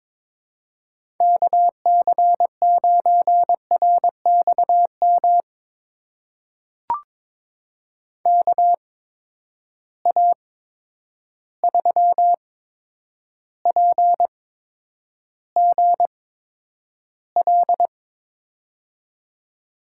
Morse code, heard as a keyed tone, about 22 wpm, spaced as for 9 wpm, 700 Hz. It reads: KC9RXM KA3PGL
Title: USA Call Signs - 6 Characters - 8x Character Spacing - 22wpm